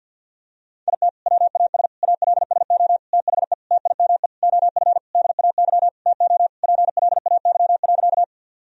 Morse code, heard as a keyed tone, about 48 wpm, 700 Hz. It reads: IT WAS ALSO THE TIME OF DAY TO PLAY.
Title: Sentences from Top 1000 Words - Review Part 1 - 50wpm